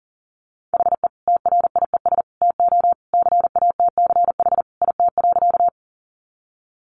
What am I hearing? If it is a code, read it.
Morse code, heard as a keyed tone, about 40 wpm, 700 Hz. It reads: HE TRIES TO CATCH IT.
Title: Additional Easy Phrases 40wpm